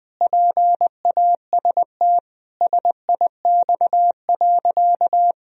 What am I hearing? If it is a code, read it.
Morse code, heard as a keyed tone, about 21 wpm, 700 Hz. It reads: PAST SIX.
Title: Sentences from Top 1000 Words - Review Part 3 - 20wpm